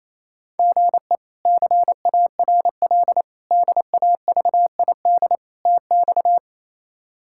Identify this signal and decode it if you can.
Morse code, heard as a keyed tone, about 28 wpm, 700 Hz. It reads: GE CARL DAVID TX
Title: K1USN SST QSO Practice - 28wpm